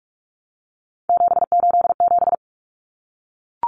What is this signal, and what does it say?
Morse code, heard as a keyed tone, about 44 wpm, 700 Hz. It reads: 787
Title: Common Cut Numbers - 3-Digit 45wpm